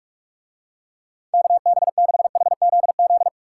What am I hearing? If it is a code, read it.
Morse code, heard as a keyed tone, about 47 wpm, 700 Hz. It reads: KB6HZZ